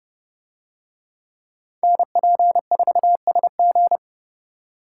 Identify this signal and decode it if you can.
Morse code, heard as a keyed tone, about 29 wpm, 700 Hz. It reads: NP4SG